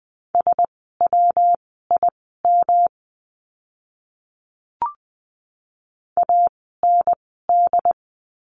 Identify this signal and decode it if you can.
Morse code, heard as a keyed tone, about 20 wpm, spaced as for 15 wpm, 700 Hz. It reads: SWIM AND
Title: 115 Lesson 19f - Letter w - Words 20wpm